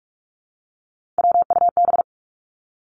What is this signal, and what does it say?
Morse code, heard as a keyed tone, about 46 wpm, 700 Hz. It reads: WUB
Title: Three Challenging Letter-Number Combinations 45wpm